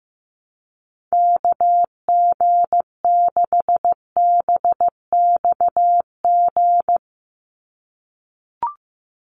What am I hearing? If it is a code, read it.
Morse code, heard as a keyed tone, about 15 wpm, 700 Hz. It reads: KG6BXG
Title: USA Call Signs 15wpm